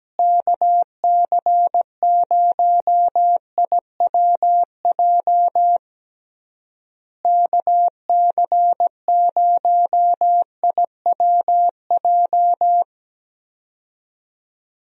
Morse code, heard as a keyed tone, about 17 wpm, 700 Hz. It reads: KC0IWJ KC0IWJ
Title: USA Call Signs - 6 Characters - Repeated 3x - 17wpm